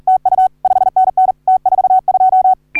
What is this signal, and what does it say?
Morse code, heard as a keyed tone, about 42 wpm, 750 Hz. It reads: TU 5NN T42
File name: CW / HA3OU RHOBCW REC